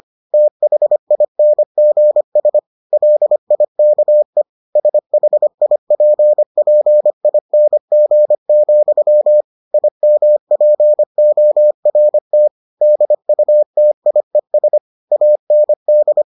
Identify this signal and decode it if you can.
Morse code, heard as a keyed tone, about 25 wpm, 600 Hz. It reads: THINGS LIKE SHIPPING, IMPORT DUTIES AND